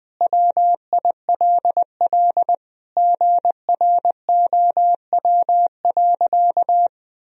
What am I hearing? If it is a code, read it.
Morse code, heard as a keyed tone, about 20 wpm, 700 Hz. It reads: WILL GROW.
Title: Sentences from Top 1000 Words - Review Part 2 - 20wpm